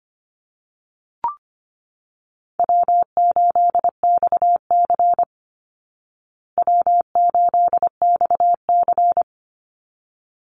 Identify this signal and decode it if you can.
Morse code, heard as a keyed tone, about 25 wpm, 700 Hz. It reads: W8XC W8XC
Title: Easy Call Signs Repeated 2x - 25wpm